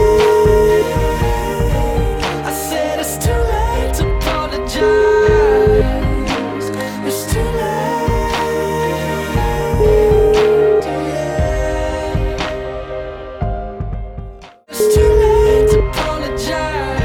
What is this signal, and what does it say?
Ringback tone (Europe, 425 Hz), with noise.